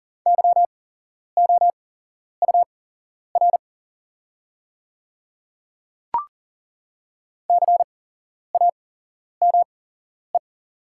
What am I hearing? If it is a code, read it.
Morse code, heard as a keyed tone, about 41 wpm, spaced as for 11 wpm, 700 Hz. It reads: YOUR CAME
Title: 8x Character Spacing - Top 300 Words - 40wpm